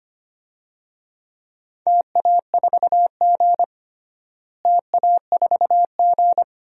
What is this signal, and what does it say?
Morse code, heard as a keyed tone, about 25 wpm, 700 Hz. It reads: TA4G TA4G
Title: DXCC-100 Call Signs - 4 Characters - Repeated 2x - 25wpm